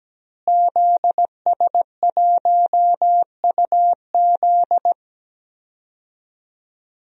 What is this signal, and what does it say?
Morse code, heard as a keyed tone, about 17 wpm, 700 Hz. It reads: ZS1UZ
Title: DXCC-100 Call Signs - 5 Characters - Repeated 2x - 17wpm